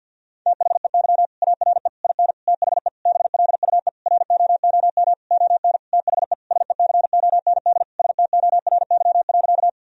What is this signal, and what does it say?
Morse code, heard as a keyed tone, about 50 wpm, 700 Hz. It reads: THEY ARE IN THE BLUE ROOM ON THE SECOND STORY.
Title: Sentences from Top 1000 Words - Review Part 3 - 50wpm